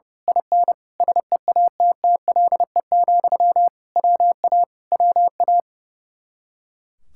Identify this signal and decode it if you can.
Morse code, heard as a keyed tone, about 28 wpm, 700 Hz. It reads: IN SEATTLE, WA WA
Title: Speed-Racing Common QSO Phrases - 30wpm